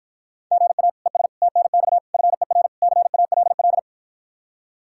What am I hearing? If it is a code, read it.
Morse code, heard as a keyed tone, about 53 wpm, 700 Hz. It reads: GA ES TNX FER CALL